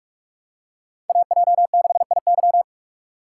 Morse code, heard as a keyed tone, about 46 wpm, 700 Hz. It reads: AJ6IY
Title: USA Call Signs - 5 Characters - 3x Character Spacing - 45wpm